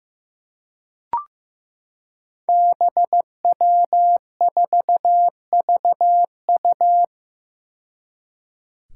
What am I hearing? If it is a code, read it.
Morse code, heard as a keyed tone, about 15 wpm, 700 Hz. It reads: BW4VU
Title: Five Challenging Letter-Number Combinations 15wpm